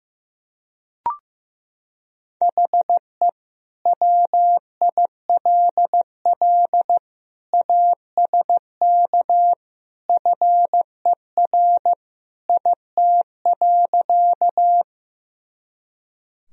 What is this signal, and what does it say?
Morse code, heard as a keyed tone, about 16 wpm, 700 Hz. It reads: HE WILL ASK FER IT.
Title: Sentences from Top 200 Words and Abbreviations 15wpm